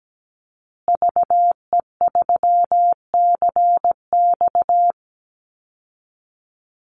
Morse code, heard as a keyed tone, about 17 wpm, 700 Hz. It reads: VE3CX
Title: Call Signs Repeated 2x - 17wpm